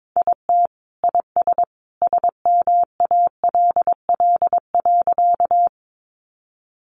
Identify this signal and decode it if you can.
Morse code, heard as a keyed tone, about 22 wpm, 700 Hz. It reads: IT IS SMALL.
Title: Additional Easy Phrases 22wpm